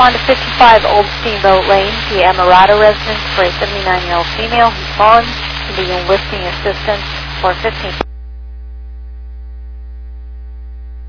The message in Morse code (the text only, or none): none